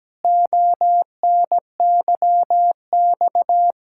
Morse code, heard as a keyed tone, about 17 wpm, 700 Hz. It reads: ONYX
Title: Pangram Challenge v2 - 17wpm